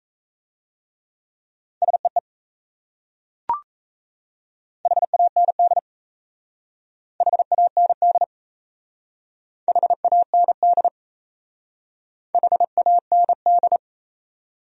Morse code, heard as a keyed tone, about 34 wpm, 700 Hz. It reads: SEE HAND HAND HAND HAND
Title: Speed-Racing Top 500 Words 28wpm